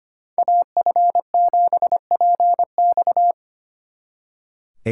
Morse code, heard as a keyed tone, about 30 wpm, 700 Hz. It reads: AF7PX
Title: Speed Racing - USA 2x2 Call Signs 25wpm